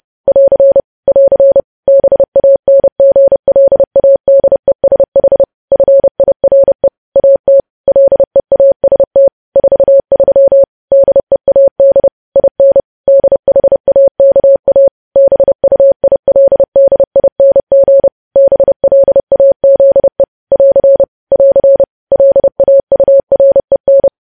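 Morse code, heard as a keyed tone, about 30 wpm, 550 Hz. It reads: <AR> <AR> BANGLADESH FIRE AT LEAST 43 DEAD IN DHAKA BUILDING BLAZE <AR> <AR> LAUREN
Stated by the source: Morse code sending of news headlines at 30 wpm.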